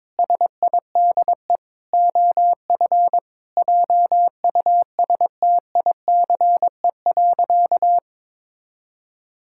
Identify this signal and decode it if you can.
Morse code, heard as a keyed tone, about 22 wpm, 700 Hz. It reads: SIDE OF JUSTICE.
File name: practice / 22wpm / Sentences with 4-Word Phrases - 22wpm